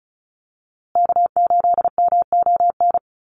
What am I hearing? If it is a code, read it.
Morse code, heard as a keyed tone, about 34 wpm, 700 Hz. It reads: K8MON
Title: USA Call Signs Repeated 2x 35wpm